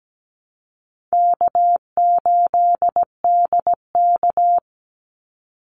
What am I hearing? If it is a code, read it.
Morse code, heard as a keyed tone, about 17 wpm, 700 Hz. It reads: K8DK
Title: Call Signs - 17wpm